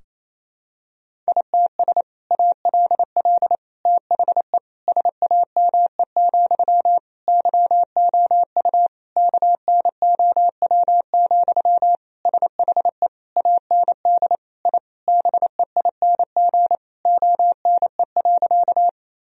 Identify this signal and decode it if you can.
Morse code, heard as a keyed tone, about 28 wpm, 700 Hz. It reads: ITS ALL THE SAME, YOU KNOW, SHE AND I BEING ONE.
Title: Sentences from Top 1000 Words - Review Part 3 - 28wpm